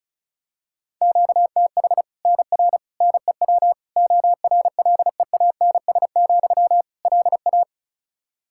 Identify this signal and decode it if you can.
Morse code, heard as a keyed tone, about 36 wpm, 700 Hz. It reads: QTH NR NEW ORLEANS, LA
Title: K1USN SST QSO Practice 35wpm